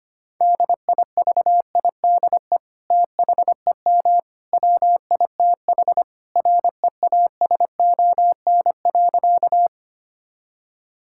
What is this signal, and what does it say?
Morse code, heard as a keyed tone, about 25 wpm, 700 Hz. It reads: DIVIDE THEM WITH REASON.
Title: Sentences from Top 1600 Words 25wpm